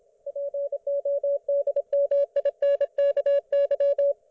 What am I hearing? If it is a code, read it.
Morse code, heard as a keyed tone, about 27 wpm, 560 Hz. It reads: PODMINKY